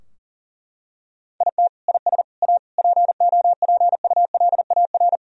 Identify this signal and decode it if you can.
Morse code, heard as a keyed tone, about 40 wpm, 700 Hz. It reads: IT IS A POPULAR
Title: Sentences from Top 1000 Words - Review Part 3 - 40wpm